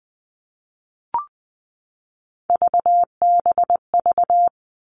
Morse code, heard as a keyed tone, about 20 wpm, 700 Hz. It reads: VBV
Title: All Cut Numbers - 3-Digit 20wpm